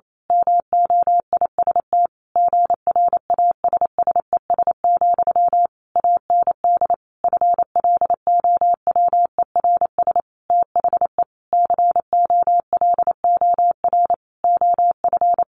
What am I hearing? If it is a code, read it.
Morse code, heard as a keyed tone, about 28 wpm, 700 Hz. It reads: MOIST GRASSES, AND FLOWERS THE COLOR OF
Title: A Sound Of Thunder 28wpm